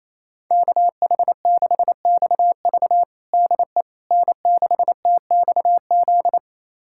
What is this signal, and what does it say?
Morse code, heard as a keyed tone, about 28 wpm, 700 Hz. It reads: KH6XV DE N6TXZ
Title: USA Callsign DE Callsign - 5 Characters - 28wpm